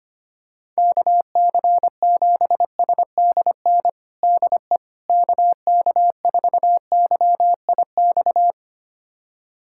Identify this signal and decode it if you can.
Morse code, heard as a keyed tone, about 25 wpm, 700 Hz. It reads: KC7SDN DE KK4YIX
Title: USA Callsign DE Callsign - 6 Characters - 25wpm